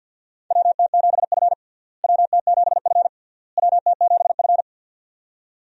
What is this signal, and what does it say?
Morse code, heard as a keyed tone, about 50 wpm, 700 Hz. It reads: WT7F WT7F WT7F
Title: USA Call Signs - 4 Characters - Repeated 3x - 50wpm